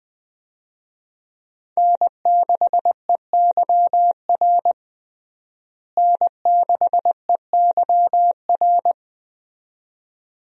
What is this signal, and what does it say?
Morse code, heard as a keyed tone, about 20 wpm, 700 Hz. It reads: N6EYR N6EYR
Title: USA Call Signs - 5 Characters - Repeated 2x - 20wpm